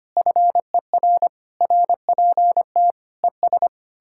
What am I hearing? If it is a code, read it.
Morse code, heard as a keyed tone, about 32 wpm, 700 Hz. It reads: FER RPT ES